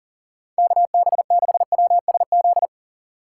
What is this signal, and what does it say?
Morse code, heard as a keyed tone, about 40 wpm, 700 Hz. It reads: KB6WSZ
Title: USA Call Signs - 6 Characters - 6x Character Spacing - 40wpm